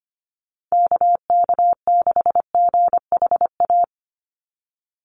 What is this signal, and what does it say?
Morse code, heard as a keyed tone, about 25 wpm, 700 Hz. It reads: KK6GHA
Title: Call Signs - 25wpm